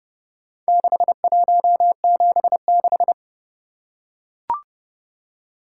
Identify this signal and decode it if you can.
Morse code, heard as a keyed tone, about 29 wpm, 700 Hz. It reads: 6176